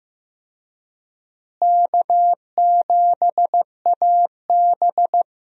Morse code, heard as a keyed tone, about 15 wpm, 700 Hz. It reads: K7AB
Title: USA Call Signs - 4 Characters - Repeated 2x - 15wpm